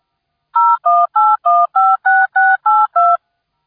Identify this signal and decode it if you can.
DTMF, heard as telephone touch tones, tones of about 220 ms. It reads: 018156682